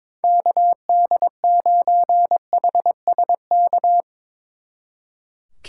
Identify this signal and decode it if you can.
Morse code, heard as a keyed tone, about 23 wpm, 700 Hz. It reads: KD9HSK
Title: Speed Racing - USA 2x3 Call Signs 22wpm